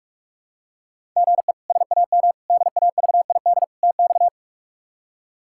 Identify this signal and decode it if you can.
Morse code, heard as a keyed tone, about 46 wpm, 700 Hz. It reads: GE SAM DAVID TX
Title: K1USN SST QSO Practice - 45wpm